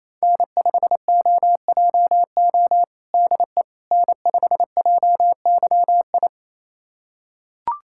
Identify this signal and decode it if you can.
Morse code, heard as a keyed tone, about 27 wpm, 700 Hz. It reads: N5OJO DE N5JYI
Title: USA Callsign DE Callsign - 5 Characters - 28wpm